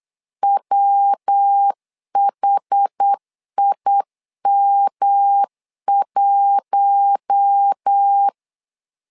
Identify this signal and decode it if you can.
Morse code, heard as a keyed tone, about 8 wpm, 800 Hz. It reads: WHIM1
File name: LEZIONI / 4 Q1 8wpm 3035